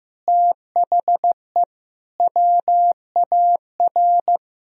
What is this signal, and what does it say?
Morse code, heard as a keyed tone, about 15 wpm, 700 Hz. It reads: THE WAR
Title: Speed-Racing - Sets of 5 Words 15wpm